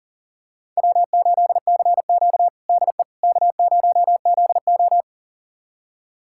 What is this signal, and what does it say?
Morse code, heard as a keyed tone, about 40 wpm, 700 Hz. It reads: W8CQ DE K0ZO
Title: USA Callsign DE Callsign - 4 Characters - 40wpm